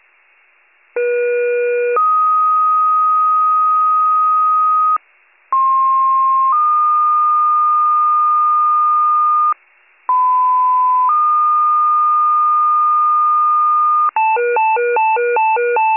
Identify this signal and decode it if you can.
Two-tone page, heard as a one-second tone, then a three-second tone, about 480 then 1250 Hz, about 1060 then 1220 Hz, about 1010 then 1220 Hz. It is heard from 1.0 to 14.1 s.